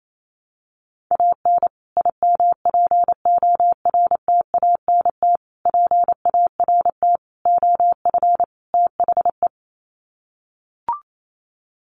Morse code, heard as a keyed tone, about 28 wpm, 700 Hz. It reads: AN IMPORTANT PART OF THE
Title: Sets of 5 Words Encore 28wpm